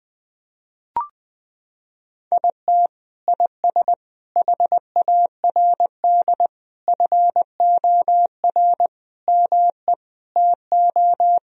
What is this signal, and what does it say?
Morse code, heard as a keyed tone, about 20 wpm, 700 Hz. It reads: IT IS HARD FOR ME TO
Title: Sentences from Top 1000 Words - Review Part 3 - 20wpm